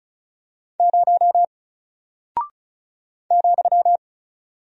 Morse code, heard as a keyed tone, about 36 wpm, 700 Hz. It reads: 0 ,
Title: Single Character (Letter:Number:Punctuation:Prosign) Alt 35wpm